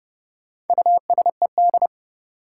Morse code, heard as a keyed tone, about 30 wpm, 700 Hz. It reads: USED
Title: Sentences from Top 600 Words - New Word Drill - 30wpm